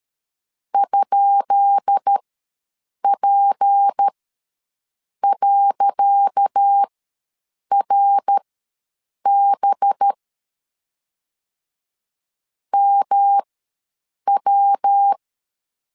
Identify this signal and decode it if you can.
Morse code, heard as a keyed tone, about 13 wpm, spaced as for 7 wpm, 800 Hz. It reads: ?P.RB MW
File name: LEZIONI / 10 SIGNS / 10C DOT ws 12wpm 3548